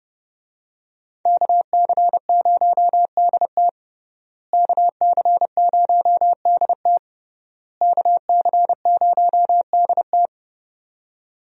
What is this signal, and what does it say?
Morse code, heard as a keyed tone, about 31 wpm, 700 Hz. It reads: KC0DT KC0DT KC0DT
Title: USA Call Signs - 5 Characters - Repeated 3x - 30wpm